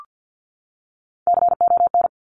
Morse code, heard as a keyed tone, about 50 wpm, 700 Hz. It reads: CON
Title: Top Three Letter Combinations in Words 50wpm